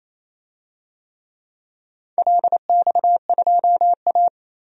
Morse code, heard as a keyed tone, about 27 wpm, 700 Hz. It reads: LX2A
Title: DXCC-100 Call Signs - 4 Characters - 28wpm